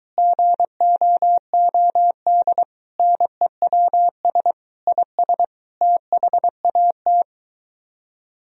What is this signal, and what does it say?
Morse code, heard as a keyed tone, about 23 wpm, 700 Hz. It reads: GOOD NEWS IS THAT